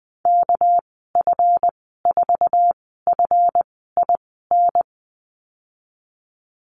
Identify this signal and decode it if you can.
Morse code, heard as a keyed tone, about 19 wpm, spaced as for 14 wpm, 700 Hz. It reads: KF4FIN
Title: 231 Lesson 37g - Letter x - Call Signs 20wpm